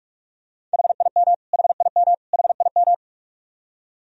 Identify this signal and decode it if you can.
Morse code, heard as a keyed tone, about 46 wpm, 700 Hz. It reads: HIM HIM HIM
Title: Top 100 Words Repeated 45wpm